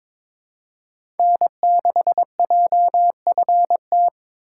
Morse code, heard as a keyed tone, about 21 wpm, 700 Hz. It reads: N6JFT
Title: USA Call Signs - 5 Characters - 2x Character Spacing - 22wpm